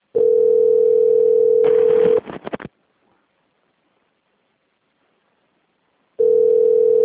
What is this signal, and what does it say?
Ringback tone (North America), 440 + 480 Hz.